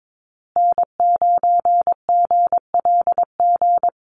Morse code, heard as a keyed tone, about 22 wpm, 700 Hz. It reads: N9GLG
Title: USA Call Signs Repeated 2x 22wpm